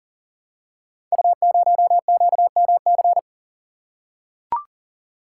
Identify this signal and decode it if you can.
Morse code, heard as a keyed tone, about 38 wpm, 700 Hz. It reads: U0QMC
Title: Code Groups - 5 Letter:Numbers - 40wpm